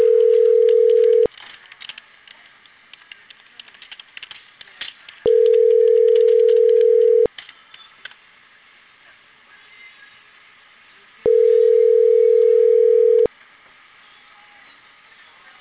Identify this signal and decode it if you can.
Ringback tone (North America), 440 + 480 Hz.